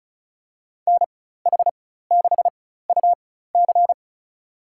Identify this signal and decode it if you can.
Morse code, heard as a keyed tone, about 36 wpm, spaced as for 16 wpm, 700 Hz. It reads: NH6UC
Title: USA Call Signs - 4x Character Spacing - 35wpm